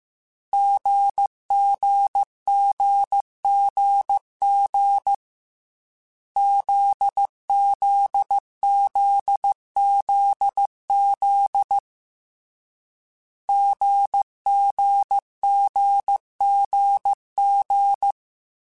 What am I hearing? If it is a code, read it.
Morse code, heard as a keyed tone, about 15 wpm, 800 Hz. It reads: GGGGG ZZZZZ GGGGG
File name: Difficult Combos / Difficult Combos-mp3s / GZ GZ7 @13